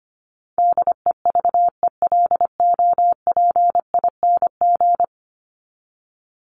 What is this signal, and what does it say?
Morse code, heard as a keyed tone, about 25 wpm, 700 Hz. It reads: DEVELOPING
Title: ING Suffix - Top 100 Words with Base Words 25wpm